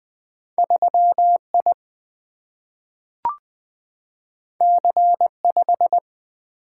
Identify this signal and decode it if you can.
Morse code, heard as a keyed tone, about 21 wpm, 700 Hz. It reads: 3 I C 5
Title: Single-Letter Single-Number 20wpm